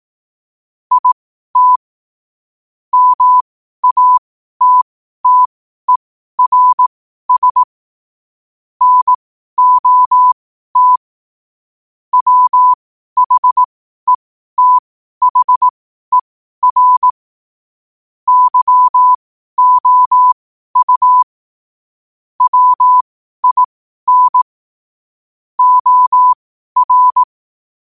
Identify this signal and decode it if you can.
Morse code, heard as a keyed tone, about 18 wpm, spaced as for 12 wpm, 1000 Hz. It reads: IT MATTERS NOT WHETHER YOU WIN OR
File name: qotdfiles / 12wpm